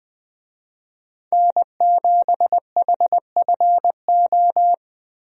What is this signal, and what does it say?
Morse code, heard as a keyed tone, about 20 wpm, 700 Hz. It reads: N7HFO
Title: USA Call Signs - 5 Characters - Repeated 3x - 20wpm